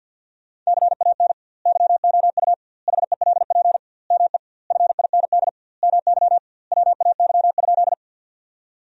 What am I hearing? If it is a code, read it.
Morse code, heard as a keyed tone, about 50 wpm, 700 Hz. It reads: CAN YOU HELP ME FIND MY WAY?
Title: 2x Word-Spacing - Sentences from Top 200 Words - 50wpm